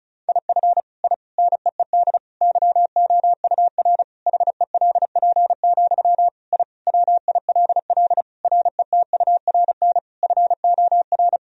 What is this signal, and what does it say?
Morse code, heard as a keyed tone, about 36 wpm, 700 Hz. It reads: IF I NEED YOUR HELP, I WILL RETURN FOR
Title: Sentences from Top 1000 Words - Review Part 1 - 35wpm